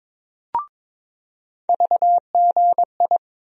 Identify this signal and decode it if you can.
Morse code, heard as a keyed tone, about 22 wpm, 700 Hz. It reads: VGI
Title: USA Call Sign Postfixes - 3 Letters 22wpm